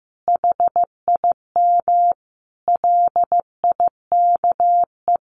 Morse code, heard as a keyed tone, about 15 wpm, 700 Hz. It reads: HIM LIKE
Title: Sentences from Top 1300 Words 15wpm